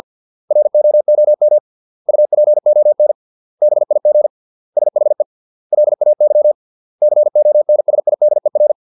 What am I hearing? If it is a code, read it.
Morse code, heard as a keyed tone, about 50 wpm, 600 Hz. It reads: ROOM UPON BIG SHE LAY CONSIDER